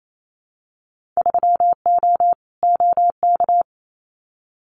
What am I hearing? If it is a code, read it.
Morse code, heard as a keyed tone, about 28 wpm, 700 Hz. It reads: 3O OK
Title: Winter Field Day Exchanges Encore 28wpm